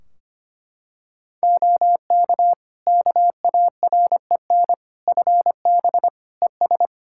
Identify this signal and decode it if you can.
Morse code, heard as a keyed tone, about 25 wpm, 700 Hz. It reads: OK KAREN FB ES
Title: Practice QSO - Exchange 2 - Info - 25wpm